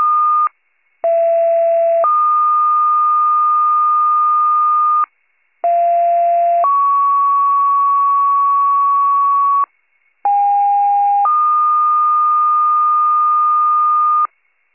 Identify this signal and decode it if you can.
Two-tone page, heard as a one-second tone, then a three-second tone, about 670 then 1220 Hz, about 690 then 1120 Hz, about 810 then 1220 Hz. It is heard from 1.0 to 14.3 s.
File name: pager / 39 RESCUE / wcalert